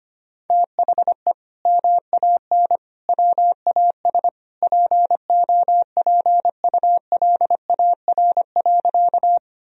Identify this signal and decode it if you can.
Morse code, heard as a keyed tone, about 25 wpm, 700 Hz. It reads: THE MAN WAS POPULAR.
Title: Sentences from Top 1000 Words - Review Part 3 - 25wpm